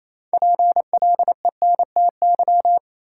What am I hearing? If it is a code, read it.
Morse code, heard as a keyed tone, about 27 wpm, 700 Hz. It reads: PLENTY
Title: Sentences from Top 1300 Words - New Word Drill Alt - 28wpm